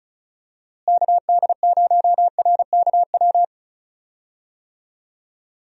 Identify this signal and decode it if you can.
Morse code, heard as a keyed tone, about 36 wpm, 700 Hz. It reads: KD0RKW
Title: USA Call Signs - 6 Characters - Repeated 3x - 35wpm